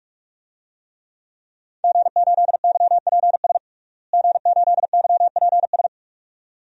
Morse code, heard as a keyed tone, about 46 wpm, 700 Hz. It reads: G8YPS G8YPS
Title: DXCC-100 Call Signs - 5 Characters - Repeated 2x - 45wpm